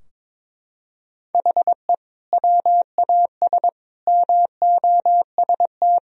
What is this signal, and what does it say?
Morse code, heard as a keyed tone, about 22 wpm, 700 Hz. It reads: HE WAS MOST
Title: Sentences from Top 1000 Words - Review Part 4 - 22wpm